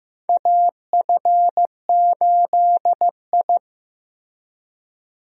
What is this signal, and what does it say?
Morse code, heard as a keyed tone, about 15 wpm, 700 Hz. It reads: AF8I
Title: USA Call Signs - 4 Characters - Repeated 3x - 15wpm